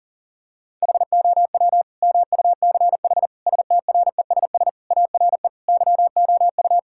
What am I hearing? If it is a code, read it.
Morse code, heard as a keyed tone, about 40 wpm, 700 Hz. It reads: HOW MUCH STRESS ARE YOU
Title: Sentences from Top 1000 Words - Review Part 6 - 40wpm